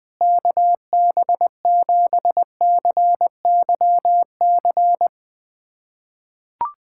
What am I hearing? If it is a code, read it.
Morse code, heard as a keyed tone, about 20 wpm, 700 Hz. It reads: KB7CYC
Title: USA Callsign DE Callsign - 6 Characters - 20wpm